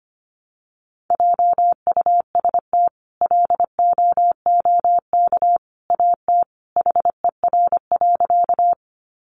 Morse code, heard as a keyed tone, about 25 wpm, 700 Hz. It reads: JUST LOOK AT HER.
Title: Sentences from Top 200 Words 25wpm